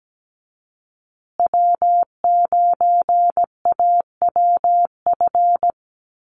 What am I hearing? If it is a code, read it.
Morse code, heard as a keyed tone, about 17 wpm, 700 Hz. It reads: W9AWF
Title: USA Call Signs Repeated 2x 17wpm